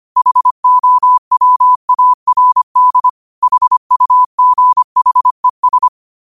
Morse code, heard as a keyed tone, about 25 wpm, 1000 Hz. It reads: HOWARD HUGHES
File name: qotdfiles / 25wpm